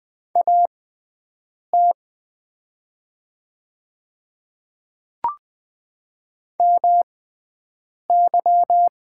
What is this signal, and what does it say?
Morse code, heard as a keyed tone, about 20 wpm, spaced as for 7 wpm, 700 Hz. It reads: AT MY
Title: 6x Character Spacing - Top 100 Words - 20wpm